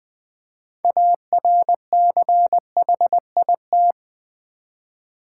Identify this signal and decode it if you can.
Morse code, heard as a keyed tone, about 20 wpm, 700 Hz. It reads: ARCHIT
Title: Build A Word - 6 Letters At A Time - 12-Letter Words - 20wpm